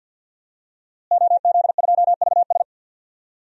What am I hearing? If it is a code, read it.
Morse code, heard as a keyed tone, about 46 wpm, 700 Hz. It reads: OZ2VS
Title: DXCC-100 Call Signs - 5 Characters - Repeated 2x - 50wpm